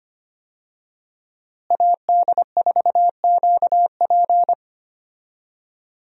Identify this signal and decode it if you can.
Morse code, heard as a keyed tone, about 25 wpm, 700 Hz. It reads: AD4QP
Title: USA Call Signs - 5 Characters - Repeated 3x - 25wpm